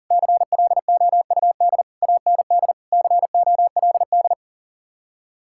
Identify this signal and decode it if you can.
Morse code, heard as a keyed tone, about 49 wpm, 700 Hz. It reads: CLOUD AND COLD
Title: Speed-Racing Common QSO Phrases Encore - 40wpm